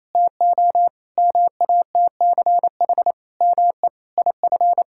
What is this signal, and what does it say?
Morse code, heard as a keyed tone, about 27 wpm, 700 Hz. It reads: TO MATCH ME IF
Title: Sentences from Top 1600 Words 28wpm